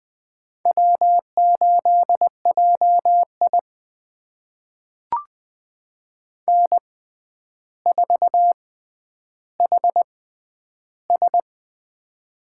Morse code, heard as a keyed tone, about 20 wpm, spaced as for 10 wpm, 700 Hz. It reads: W8JI N4HS
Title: USA Call Signs - 4 Characters - 6x Character Spacing - 20wpm